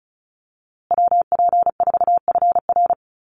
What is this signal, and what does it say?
Morse code, heard as a keyed tone, about 34 wpm, 700 Hz. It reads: WP4FR
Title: USA Call Signs Repeated 2x 35wpm